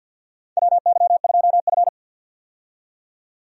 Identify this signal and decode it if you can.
Morse code, heard as a keyed tone, about 47 wpm, 700 Hz. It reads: WY2F
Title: USA Call Signs - 4 Characters - 50wpm